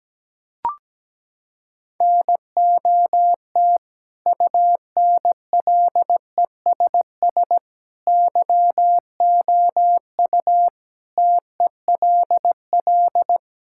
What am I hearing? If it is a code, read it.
Morse code, heard as a keyed tone, about 17 wpm, 700 Hz. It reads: NOT UNLESS YOU TELL
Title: Sentences from Top 1000 Words - Review Part 6 - 17wpm